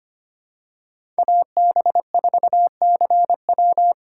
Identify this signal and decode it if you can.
Morse code, heard as a keyed tone, about 25 wpm, 700 Hz. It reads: AB4CW
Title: USA Call Signs - 5 Characters - 4x Character Spacing - 25wpm